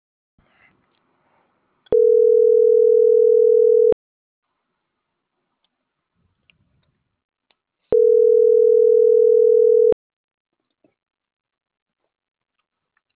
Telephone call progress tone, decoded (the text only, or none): ringback tone